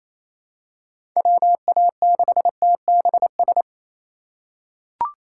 Morse code, heard as a keyed tone, about 28 wpm, 700 Hz. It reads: WA6TBS
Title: USA Call Signs - 2x Character Spacing - 28wpm